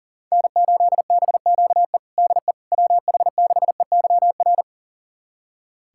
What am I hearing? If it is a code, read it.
Morse code, heard as a keyed tone, about 40 wpm, 700 Hz. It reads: N8BQE DE WH6EYR
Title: USA Callsign DE Callsign 40wpm